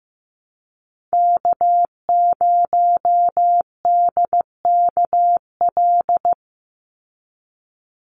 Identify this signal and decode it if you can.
Morse code, heard as a keyed tone, about 15 wpm, 700 Hz. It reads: K0DKL
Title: USA Call Signs 15wpm